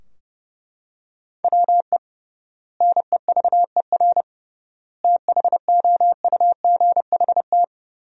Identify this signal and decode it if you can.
Morse code, heard as a keyed tone, about 30 wpm, 700 Hz. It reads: WE NEVER THOUGHT
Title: 3x Word-Spacing - Sentences from Top 300 Words - 30wpm